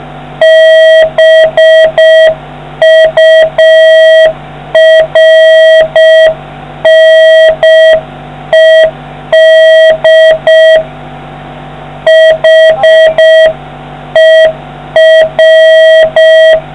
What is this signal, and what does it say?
Morse code, heard as a keyed tone, about 6 wpm, 640 Hz, with noise. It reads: BURNED HER